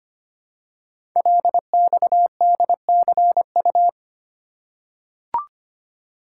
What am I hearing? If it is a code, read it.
Morse code, heard as a keyed tone, about 25 wpm, 700 Hz. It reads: LXDCU
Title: Code Groups - 5 Letters - 25wpm